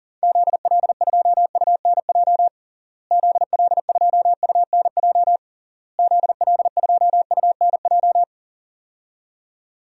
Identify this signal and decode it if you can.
Morse code, heard as a keyed tone, about 40 wpm, 700 Hz. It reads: ZL2UNJ ZL2UNJ ZL2UNJ
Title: DXCC-100 Call Signs - 6 Characters - Repeated 3x - 40wpm